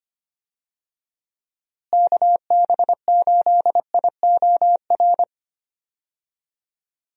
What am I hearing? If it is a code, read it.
Morse code, heard as a keyed tone, about 25 wpm, 700 Hz. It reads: KB8IOR
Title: USA Call Signs - 6 Characters - Repeated 3x - 25wpm